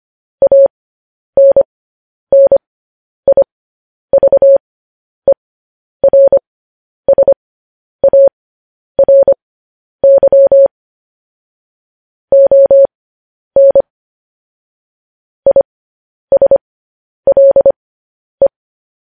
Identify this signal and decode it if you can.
Morse code, heard as a keyed tone, about 25 wpm, spaced as for 10 wpm, 550 Hz. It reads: ANNIVERSARY ON ISLE
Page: news headlines in morse code at 10 wpm